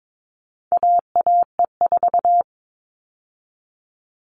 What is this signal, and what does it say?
Morse code, heard as a keyed tone, about 22 wpm, 700 Hz. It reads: AAE4
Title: All Cut Numbers - 4-Digit 22wpm